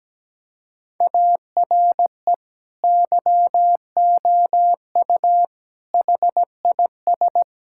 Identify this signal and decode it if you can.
Morse code, heard as a keyed tone, about 17 wpm, 700 Hz. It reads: ARE YOU HIS
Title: Sentences from Top 1600 Words 17wpm